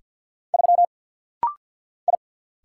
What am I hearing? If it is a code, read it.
Morse code, heard as a keyed tone, about 50 wpm, 700 Hz. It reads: 3 I
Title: Single Letter-Number - Rapid-Fire 50wpm